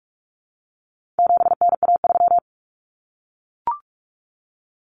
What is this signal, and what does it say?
Morse code, heard as a keyed tone, about 44 wpm, 700 Hz. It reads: 7NA3
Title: Common Cut Numbers - 4-Digit 45wpm